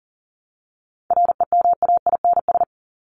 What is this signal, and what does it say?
Morse code, heard as a keyed tone, about 40 wpm, 700 Hz. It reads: REMAINS
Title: ES:S Suffix - Top 100 Words with Base Words 40wpm